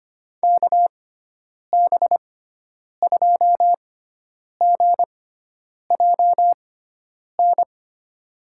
Morse code, heard as a keyed tone, about 25 wpm, spaced as for 9 wpm, 700 Hz. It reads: KB2GJN
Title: USA Call Signs - 6 Characters - 6x Character Spacing - 25wpm